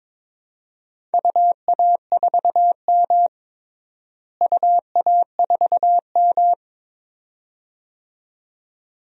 Morse code, heard as a keyed tone, about 21 wpm, 700 Hz. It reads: UA4M UA4M
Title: DXCC-100 Call Signs - 4 Characters - Repeated 2x - 22wpm